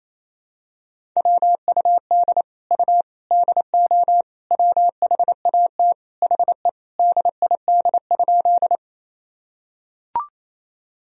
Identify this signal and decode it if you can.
Morse code, heard as a keyed tone, about 28 wpm, 700 Hz. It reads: WUD U DO WHAT HE DID?
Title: Sentences from Top 100 Words and Abbreviations 28wpm